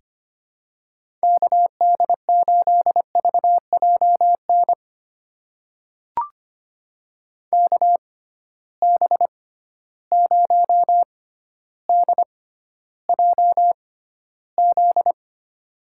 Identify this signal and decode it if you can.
Morse code, heard as a keyed tone, about 25 wpm, spaced as for 13 wpm, 700 Hz. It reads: KD8VJN KB0DJZ
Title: USA Call Signs - 6 Characters - 6x Character Spacing - 25wpm